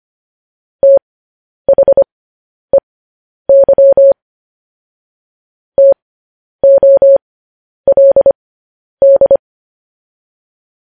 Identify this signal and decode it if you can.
Morse code, heard as a keyed tone, about 25 wpm, spaced as for 10 wpm, 550 Hz. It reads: THEY TOLD